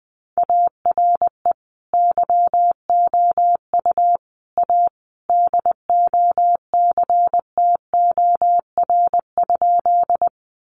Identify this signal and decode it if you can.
Morse code, heard as a keyed tone, about 20 wpm, 700 Hz. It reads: ARE YOU A DOCTOR?
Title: Sentences from Top 1500 Words 20wpm